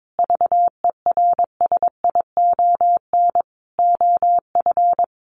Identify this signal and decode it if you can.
Morse code, heard as a keyed tone, about 22 wpm, 700 Hz. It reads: VERSION OF
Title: Sentences from Top 1400 Words 22wpm